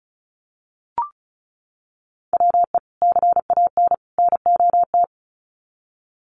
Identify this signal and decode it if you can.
Morse code, heard as a keyed tone, about 34 wpm, 700 Hz. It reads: WE CAN NOT
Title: Sets of 3 Words 35wpm